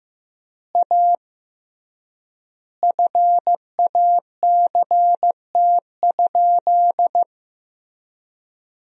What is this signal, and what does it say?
Morse code, heard as a keyed tone, about 15 wpm, 700 Hz. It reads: A FACT?
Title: 3x Word-Spacing - Sentences from Top 500 Words - 15wpm